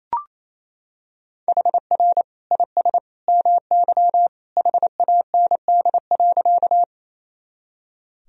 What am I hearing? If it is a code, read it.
Morse code, heard as a keyed tone, about 28 wpm, 700 Hz. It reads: HR IS MY HAND.
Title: Sentences from Top 200 Words and Abbreviations 28wpm